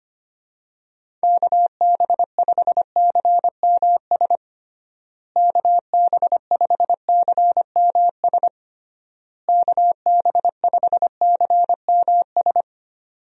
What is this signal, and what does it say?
Morse code, heard as a keyed tone, about 25 wpm, 700 Hz. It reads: KB5CMS KB5CMS KB5CMS
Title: USA Call Signs - 6 Characters - Repeated 3x - 25wpm